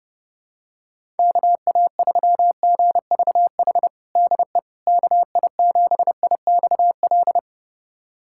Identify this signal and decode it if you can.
Morse code, heard as a keyed tone, about 30 wpm, 700 Hz. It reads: KA3GVH DE KI7IXL
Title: USA Callsign DE Callsign - 6 Characters - 30wpm